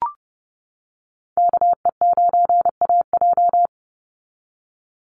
Morse code, heard as a keyed tone, about 30 wpm, 700 Hz. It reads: KE9AJ
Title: NA SOTA Chaser Callsigns 30wpm